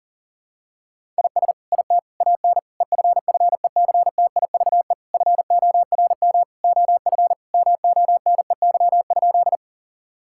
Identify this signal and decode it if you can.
Morse code, heard as a keyed tone, about 40 wpm, 700 Hz. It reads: IS IT AN EFFECTIVE FORM OF MONEY?
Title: Sentences from Top 1000 Words - Review Part 6 - 40wpm